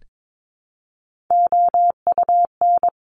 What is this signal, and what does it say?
Morse code, heard as a keyed tone, about 22 wpm, 700 Hz. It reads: OUN
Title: Top Three Letter Combinations in Words 22wpm